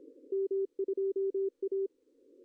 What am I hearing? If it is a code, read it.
Morse code, heard as a keyed tone, about 25 wpm, 390 Hz. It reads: M2A
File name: CW / cqww-cw-k5tr